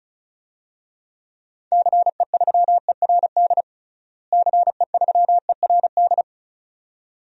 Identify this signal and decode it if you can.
Morse code, heard as a keyed tone, about 34 wpm, 700 Hz. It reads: CE3ERD CE3ERD
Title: DXCC-100 Call Signs - 6 Characters - Repeated 2x - 35wpm